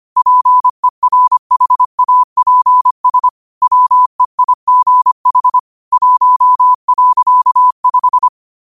Morse code, heard as a keyed tone, about 25 wpm, 1000 Hz. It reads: PERHAPS WEIGH 1.5
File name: qotdfiles / 25wpm